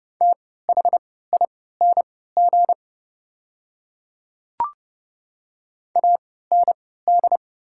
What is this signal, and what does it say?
Morse code, heard as a keyed tone, about 30 wpm, spaced as for 17 wpm, 700 Hz. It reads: THING AND
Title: 3x Character Spacing - Top 300 Words - 30wpm